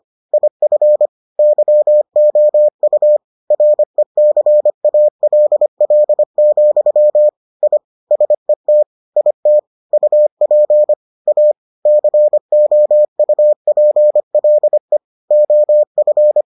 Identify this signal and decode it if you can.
Morse code, heard as a keyed tone, about 25 wpm, 600 Hz. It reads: IF YOU RECALL, I SET IT UP A COUPLE OF